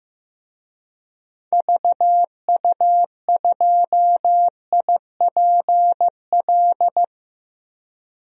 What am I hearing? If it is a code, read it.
Morse code, heard as a keyed tone, about 15 wpm, 700 Hz. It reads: VU2IPL
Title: DXCC-100 Call Signs - 6 Characters - Repeated 3x - 15wpm